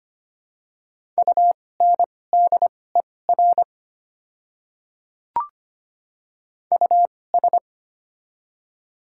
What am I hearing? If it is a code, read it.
Morse code, heard as a keyed tone, about 25 wpm, spaced as for 18 wpm, 700 Hz. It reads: UNDER US
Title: 2x Character Spacing - Top 200 Words - 25wpm